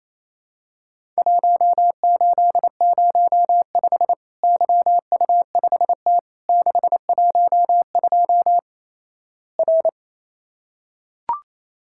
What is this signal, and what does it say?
Morse code, heard as a keyed tone, about 28 wpm, 700 Hz. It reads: 1805 YU5T 612 R
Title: Worked All Europe DX Contest - QTC Exchange - 28wpm